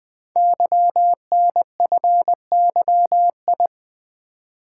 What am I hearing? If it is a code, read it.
Morse code, heard as a keyed tone, about 20 wpm, 700 Hz. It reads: YNFYI
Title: Code Groups - 5 Letters - 20wpm